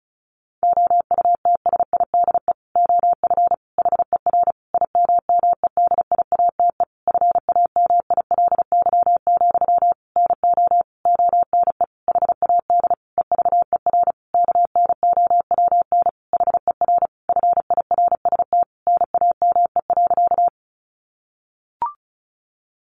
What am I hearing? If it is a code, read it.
Morse code, heard as a keyed tone, about 34 wpm, 700 Hz. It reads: OUTSIDE OF HER IMMEDIATE FAMILY, NO ONE HAD EVER KNOWN HER FIRST NAME.
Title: Sentences from Top 1200 Words 35wpm